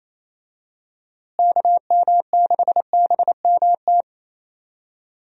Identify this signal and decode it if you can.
Morse code, heard as a keyed tone, about 28 wpm, 700 Hz. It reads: KM6BMT
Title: USA Call Signs - 6 Characters - Repeated 3x - 28wpm